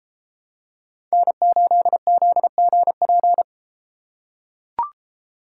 Morse code, heard as a keyed tone, about 33 wpm, 700 Hz. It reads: N8ZGP